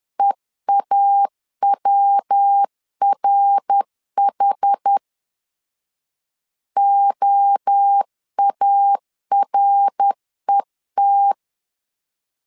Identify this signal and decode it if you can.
Morse code, heard as a keyed tone, about 10 wpm, 800 Hz. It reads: EAWRH OARET
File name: 10wpm